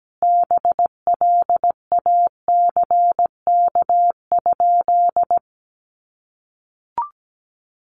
Morse code, heard as a keyed tone, about 17 wpm, 700 Hz. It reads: BLACK?
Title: Sentences from Top 400 Words 17wpm